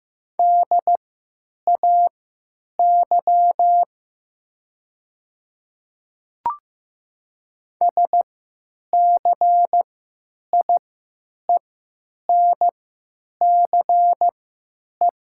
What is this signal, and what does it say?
Morse code, heard as a keyed tone, about 15 wpm, spaced as for 9 wpm, 700 Hz. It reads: DAY SCIENCE
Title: 3x Character Spacing - Top 500 Words - 15wpm